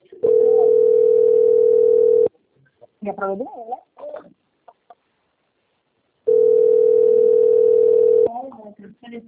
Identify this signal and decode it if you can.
Ringback tone (North America), 440 + 480 Hz, with speech.
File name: out- - - - - .13053